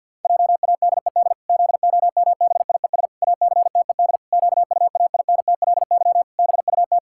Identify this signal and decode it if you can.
Morse code, heard as a keyed tone, about 50 wpm, 700 Hz. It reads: JADED ZOMBIES ACTED QUAINTLY BUT
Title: Pangram Challenge v2 - 50wpm